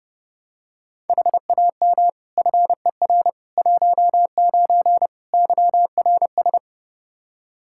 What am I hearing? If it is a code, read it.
Morse code, heard as a keyed tone, about 32 wpm, 700 Hz. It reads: HAM FER 19 YRS
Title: K1USN SST QSO Practice 30wpm